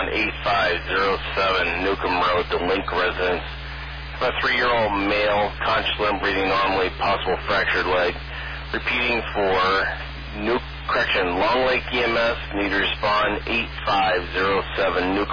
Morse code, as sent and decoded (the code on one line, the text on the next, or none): none